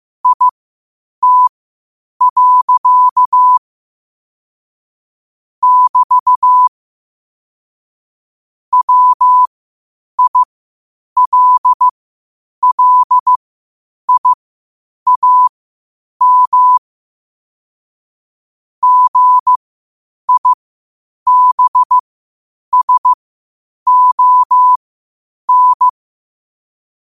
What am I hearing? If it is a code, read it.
Morse code, heard as a keyed tone, about 15 wpm, spaced as for 8 wpm, 1000 Hz. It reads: IT. <BT> WILLIAM GIBSON
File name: qotdfiles / 8wpm